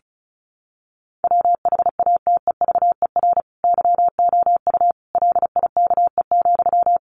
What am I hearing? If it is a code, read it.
Morse code, heard as a keyed tone, about 34 wpm, 700 Hz. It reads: WHATEVER YOU LIKE,
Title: Sentences from Top 1100 Words 35wpm